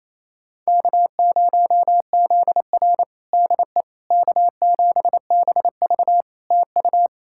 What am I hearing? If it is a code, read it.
Morse code, heard as a keyed tone, about 28 wpm, 700 Hz. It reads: K0ZR DE K7BV TU
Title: Practice QSO - Exchange 4 - Ending - 28wpm